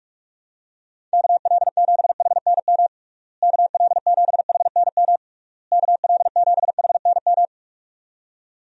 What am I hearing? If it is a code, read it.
Morse code, heard as a keyed tone, about 46 wpm, 700 Hz. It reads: KL7HNM KL7HNM KL7HNM
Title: USA Call Signs - 6 Characters - Repeated 3x - 45wpm